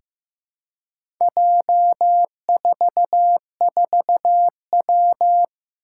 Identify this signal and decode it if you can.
Morse code, heard as a keyed tone, about 15 wpm, 700 Hz. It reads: J44W
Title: Four Challenging Letter-Number Combinations 15wpm